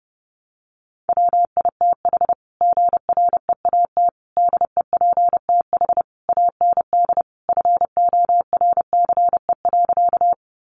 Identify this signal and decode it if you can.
Morse code, heard as a keyed tone, about 30 wpm, 700 Hz. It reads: WITH GREAT DEPTH AND FORCE.
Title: Sentences from Top 1200 Words 30wpm